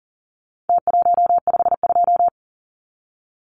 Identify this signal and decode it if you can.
Morse code, heard as a keyed tone, about 40 wpm, 700 Hz. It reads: T152
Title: Numbers - Contest Style - 4-Digit 40wpm